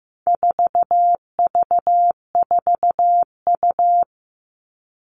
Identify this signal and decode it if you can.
Morse code, heard as a keyed tone, about 15 wpm, 700 Hz. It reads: 4V4U
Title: All Cut Numbers - 4-Digit 15wpm